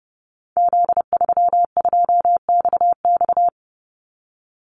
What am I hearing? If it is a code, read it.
Morse code, heard as a keyed tone, about 30 wpm, 700 Hz. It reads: Z32XX
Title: Call Signs Repeated 2x - 30wpm